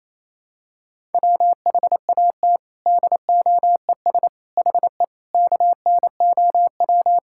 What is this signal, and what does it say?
Morse code, heard as a keyed tone, about 28 wpm, 700 Hz. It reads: WHAT DOES HE KNOW
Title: Sentences from Top 1600 Words 28wpm